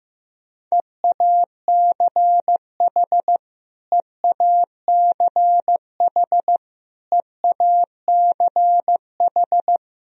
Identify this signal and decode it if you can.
Morse code, heard as a keyed tone, about 15 wpm, 700 Hz. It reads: EACH EACH EACH
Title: Top 100 Words Repeated 15wpm